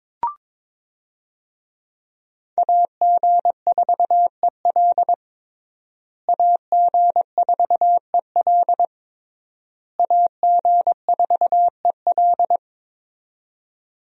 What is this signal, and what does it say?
Morse code, heard as a keyed tone, about 22 wpm, 700 Hz. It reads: AG4EL AG4EL AG4EL
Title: USA Call Signs - 5 Characters - Repeated 3x - 22wpm